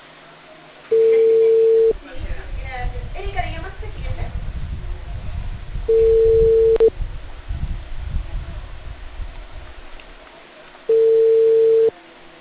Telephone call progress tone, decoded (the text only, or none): ringback tone